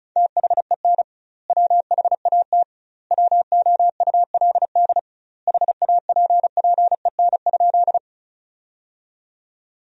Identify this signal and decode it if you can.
Morse code, heard as a keyed tone, about 36 wpm, 700 Hz. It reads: THEN WHAT WOULD HAPPEN?
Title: 2x Word-Spacing - Sentences from Top 400 Words - 35wpm